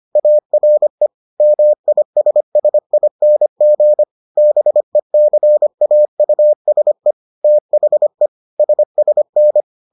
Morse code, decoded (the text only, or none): ARE MISSING BECAUSE THE SSN